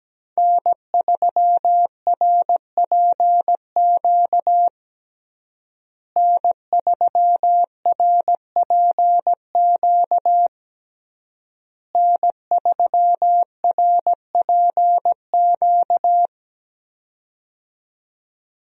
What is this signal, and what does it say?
Morse code, heard as a keyed tone, about 17 wpm, 700 Hz. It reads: N3RPQ N3RPQ N3RPQ
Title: USA Call Signs - 5 Characters - Repeated 3x - 17wpm